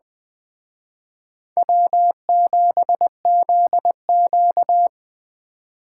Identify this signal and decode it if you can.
Morse code, heard as a keyed tone, about 21 wpm, 700 Hz. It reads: W7ZQ
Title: Speed Racing - USA Call Signs 17wpm